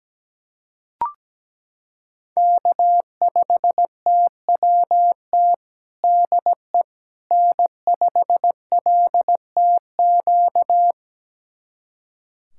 Morse code, heard as a keyed tone, about 17 wpm, 700 Hz. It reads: K5TWT DE N5LTQ
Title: USA Callsign DE Callsign - 5 Characters - 17wpm